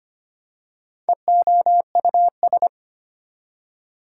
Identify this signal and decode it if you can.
Morse code, heard as a keyed tone, about 25 wpm, 700 Hz. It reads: EOUS
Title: Prefixes and Suffixes 25wpm